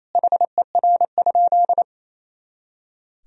Morse code, heard as a keyed tone, about 28 wpm, 700 Hz. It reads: HER?
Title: Sentences from Top 200 Words and Abbreviations 28wpm